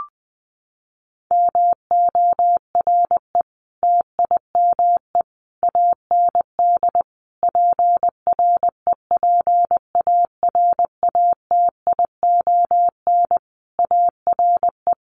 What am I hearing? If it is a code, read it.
Morse code, heard as a keyed tone, about 19 wpm, 700 Hz. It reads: MORE TIME AND PREPARATION ARE
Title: Sentences from Top 1400 Words 20wpm